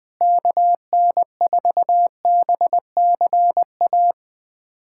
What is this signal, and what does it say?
Morse code, heard as a keyed tone, about 21 wpm, 700 Hz. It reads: KN4BCA
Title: USA Callsign DE Callsign - 6 Characters - 20wpm